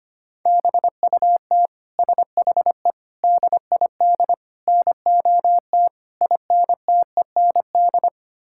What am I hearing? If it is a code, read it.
Morse code, heard as a keyed tone, about 25 wpm, 700 Hz. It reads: BUT SHE DID NOT INTEND
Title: Sentences from Top 1600 Words 25wpm